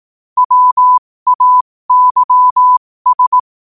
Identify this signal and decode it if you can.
Morse code, heard as a keyed tone, about 18 wpm, spaced as for 15 wpm, 1000 Hz. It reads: WAYS
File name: qotdfiles / 15wpm